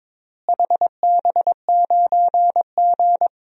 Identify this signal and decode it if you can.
Morse code, heard as a keyed tone, about 22 wpm, 700 Hz. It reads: HB9G
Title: DXCC-100 Call Signs - 4 Characters - Repeated 3x - 22wpm